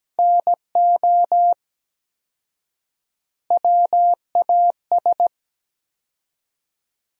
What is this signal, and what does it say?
Morse code, heard as a keyed tone, about 17 wpm, 700 Hz. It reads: NO WAS
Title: 4x Word-Spacing - Sentences from Top 100 Words - 17wpm